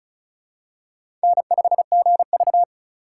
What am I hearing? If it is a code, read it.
Morse code, heard as a keyed tone, about 34 wpm, 700 Hz. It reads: N5GV
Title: USA Call Signs - 4 Characters - 4x Character Spacing - 35wpm